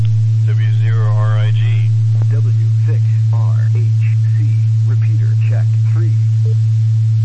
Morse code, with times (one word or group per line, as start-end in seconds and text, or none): none